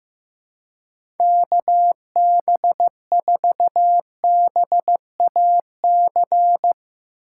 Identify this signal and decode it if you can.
Morse code, heard as a keyed tone, about 15 wpm, 700 Hz. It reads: KB4BAC
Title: USA Call Signs - 6 Characters - 4x Character Spacing - 15wpm